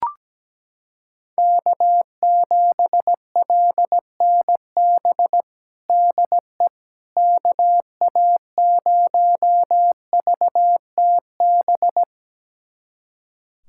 Morse code, heard as a keyed tone, about 17 wpm, 700 Hz. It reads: K7LNB DE KA0VTB
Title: USA Callsign DE Callsign 17wpm